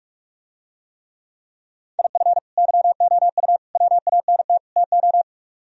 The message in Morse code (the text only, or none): IF YOU WANT TO